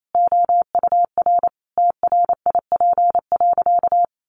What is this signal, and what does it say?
Morse code, heard as a keyed tone, about 28 wpm, 700 Hz. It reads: OUR TRIP.
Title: Sentences from Top 1300 Words 28wpm